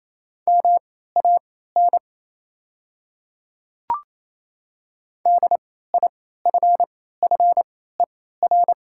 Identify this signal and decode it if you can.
Morse code, heard as a keyed tone, about 28 wpm, spaced as for 16 wpm, 700 Hz. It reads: MAN DIFFER
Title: 3x Character Spacing - Top 200 Words - 28wpm